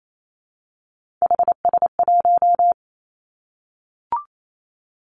Morse code, heard as a keyed tone, about 28 wpm, 700 Hz. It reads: HS1
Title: Three Challenging Letter-Number Combinations 28wpm